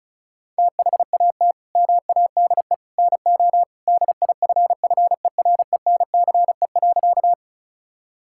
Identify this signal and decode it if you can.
Morse code, heard as a keyed tone, about 36 wpm, 700 Hz. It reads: THAT MADE NO DIFFERENCE.
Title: Sentences from Top 1000 Words - Review Part 5 - 35wpm